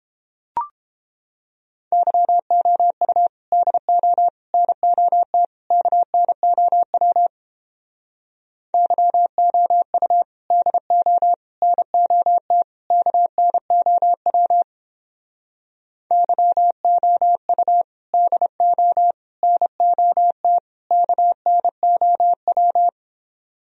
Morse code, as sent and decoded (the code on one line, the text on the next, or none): -.-- --- ..- / -.. --- / -. --- - / -.- -. --- .-- / -.-- --- ..- / -.. --- / -. --- - / -.- -. --- .-- / -.-- --- ..- / -.. --- / -. --- - / -.- -. --- .--
YOU DO NOT KNOW YOU DO NOT KNOW YOU DO NOT KNOW